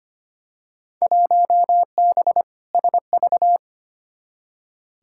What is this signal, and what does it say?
Morse code, heard as a keyed tone, about 25 wpm, 700 Hz. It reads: 1B SV
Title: Field Day Exchanges Encore - 25wpm